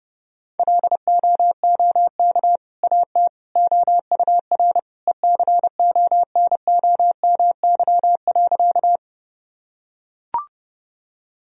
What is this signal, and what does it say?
Morse code, heard as a keyed tone, about 30 wpm, 700 Hz. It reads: LOOK AT OUR ECONOMY.
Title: Sentences from Top 1000 Words - Review Part 3 - 30wpm